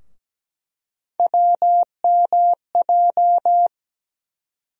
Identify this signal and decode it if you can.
Morse code, heard as a keyed tone, about 17 wpm, 700 Hz. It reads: WMJ
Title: USA Call Sign Postfixes - 3 Letters 17wpm